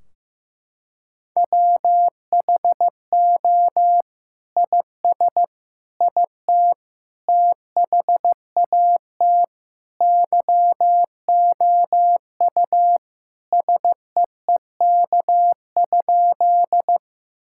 Morse code, heard as a keyed tone, about 16 wpm, 700 Hz. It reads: WHO IS IT THAT YOU SEEK?
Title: Sentences from Top 1600 Words 15wpm